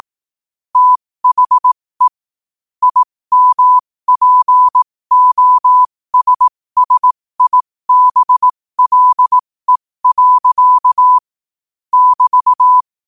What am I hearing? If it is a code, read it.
Morse code, heard as a keyed tone, about 18 wpm, spaced as for 15 wpm, 1000 Hz. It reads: THE IMPOSSIBLE. <BT>
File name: qotdfiles / 15wpm